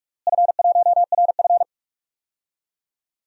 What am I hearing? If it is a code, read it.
Morse code, heard as a keyed tone, about 48 wpm, 700 Hz. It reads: F1RF
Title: DXCC-100 Call Signs - 4 Characters - Repeated 2x - 45wpm